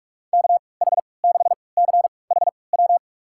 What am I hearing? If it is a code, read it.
Morse code, heard as a keyed tone, about 44 wpm, spaced as for 25 wpm, 700 Hz. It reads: KH6CHW
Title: USA Call Signs - 6 Characters - 3x Character Spacing - 45wpm